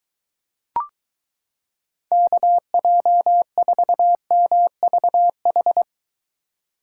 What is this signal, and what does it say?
Morse code, heard as a keyed tone, about 22 wpm, 700 Hz. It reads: KJ4MVH